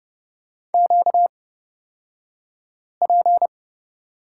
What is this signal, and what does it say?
Morse code, heard as a keyed tone, about 30 wpm, spaced as for 5 wpm, 700 Hz. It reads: QP
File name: Lesson 38d - Letter q - Two Characters 30wpm